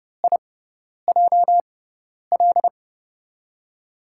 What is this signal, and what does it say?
Morse code, heard as a keyed tone, about 30 wpm, 700 Hz. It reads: I J L
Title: USA Call Signs - 5 Characters - 6x Character Spacing - 30wpm